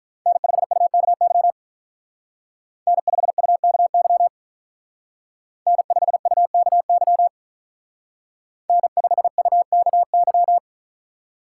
Morse code, heard as a keyed tone, about 43 wpm, 700 Hz. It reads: N5UKY N5UKY N5UKY N5UKY
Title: Speed Racing - USA 1x3 Call Signs 35wpm